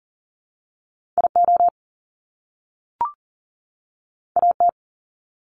Morse code, heard as a keyed tone, about 41 wpm, 700 Hz. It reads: IO AT
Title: Top Two Letter Combinations in Words 40wpm